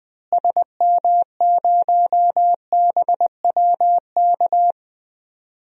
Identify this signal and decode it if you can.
Morse code, heard as a keyed tone, about 20 wpm, 700 Hz. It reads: SM0BWK
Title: DXCC-100 Call Signs - 6 Characters - Repeated 3x - 20wpm